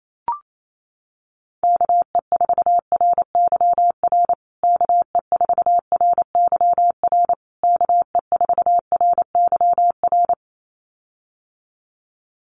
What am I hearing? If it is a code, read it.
Morse code, heard as a keyed tone, about 28 wpm, 700 Hz. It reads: KE4RYR KE4RYR KE4RYR
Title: USA Call Signs Repeated 3x 28wpm